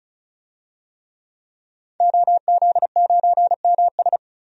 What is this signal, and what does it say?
Morse code, heard as a keyed tone, about 36 wpm, 700 Hz. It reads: OZ9MS
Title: DXCC-100 Call Signs - 5 Characters - 35wpm